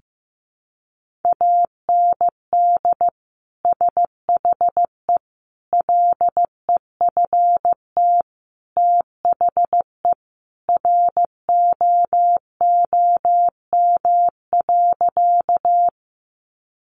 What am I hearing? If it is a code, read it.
Morse code, heard as a keyed tone, about 15 wpm, 700 Hz. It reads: AND SHE LEFT THE ROOM.
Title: Sentences from Top 300 Words 15wpm